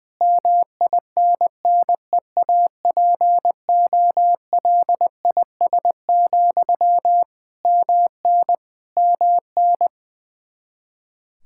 Morse code, heard as a keyed tone, about 20 wpm, 700 Hz. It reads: MINNEAPOLIS, MN MN
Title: Practice QSO - Exchange 1 - Beginning - 20wpm